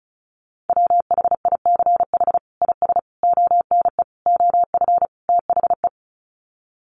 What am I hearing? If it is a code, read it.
Morse code, heard as a keyed tone, about 34 wpm, 700 Hz. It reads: WHICH IS ONE OF THE
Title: Sets of 5 Words Encore 35wpm